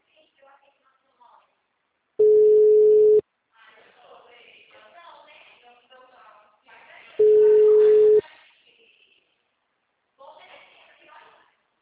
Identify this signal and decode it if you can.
Ringback tone (Europe, 425 Hz), with speech.